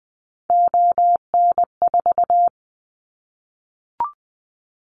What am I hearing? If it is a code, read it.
Morse code, heard as a keyed tone, about 20 wpm, 700 Hz. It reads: ON4
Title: Callsign Prefixes for Top 25 DXCC - Repeated 3x - 20wpm